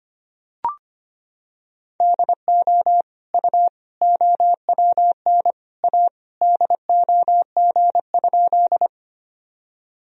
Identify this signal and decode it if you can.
Morse code, heard as a keyed tone, about 25 wpm, 700 Hz. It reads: DO U OWN A DOG?
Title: Sentences from Top 400 Words and Abbreviations 25wpm